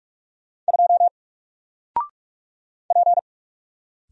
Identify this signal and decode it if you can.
Morse code, heard as a keyed tone, about 46 wpm, 700 Hz. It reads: 2 P
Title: Single Letter-Number Alt 45wpm